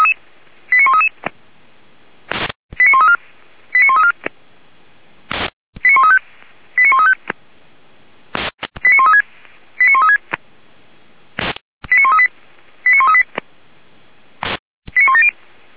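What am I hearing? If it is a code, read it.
Five-tone selective call (ZVEI), heard as quick runs of five tones, tones of about 70 ms. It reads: 89133 89134 89134 89135 89135 89136 89136 89138 89138 89170
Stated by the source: ZVEI five-tone selective call received by SDR, from audio.f85360.feuerwehr-orlamuende.org.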